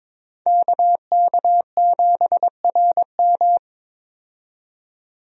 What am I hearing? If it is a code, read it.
Morse code, heard as a keyed tone, about 22 wpm, 700 Hz. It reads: KK7RM
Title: USA Call Signs - 5 Characters - 22wpm